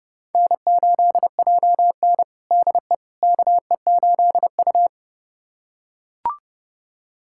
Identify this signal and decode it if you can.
Morse code, heard as a keyed tone, about 32 wpm, 700 Hz. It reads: N8JN DE KE8U
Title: USA Callsign DE Callsign - 4 Characters - 30wpm